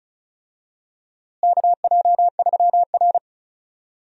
Morse code, heard as a keyed tone, about 36 wpm, 700 Hz. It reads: KJ3R
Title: USA Call Signs - 4 Characters - Repeated 3x - 35wpm